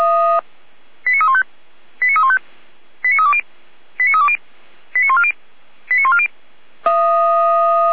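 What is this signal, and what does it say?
Five-tone selective call (ZVEI), heard as quick runs of five tones, tones of about 70 ms. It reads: 89315 89315 89320 89320 89140 89140
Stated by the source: ZVEI five-tone selective call received with an SDR, hosted by Feuerwehr Orlamünde (fire brigade).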